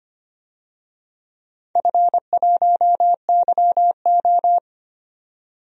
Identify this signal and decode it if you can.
Morse code, heard as a keyed tone, about 25 wpm, 700 Hz. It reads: F1YO
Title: DXCC-100 Call Signs - 4 Characters - 25wpm